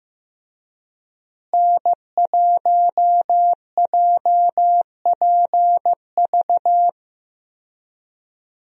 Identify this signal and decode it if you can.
Morse code, heard as a keyed tone, about 15 wpm, 700 Hz. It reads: N1JPV
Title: USA Call Signs - 5 Characters - Repeated 3x - 15wpm